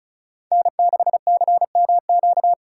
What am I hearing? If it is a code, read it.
Morse code, heard as a keyed tone, about 42 wpm, 700 Hz. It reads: N6CMQ
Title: Speed Racing - USA 1x3 Call Signs 35wpm